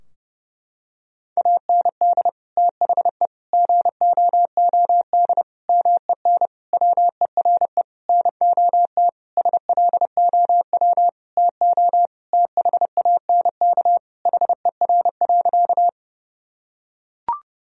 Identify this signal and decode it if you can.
Morse code, heard as a keyed tone, about 32 wpm, 700 Hz. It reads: AND THE GOOD MEN WERE NOT SLOW TO THANK HER.
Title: Sentences from Top 1000 Words - Review Part 2 - 30wpm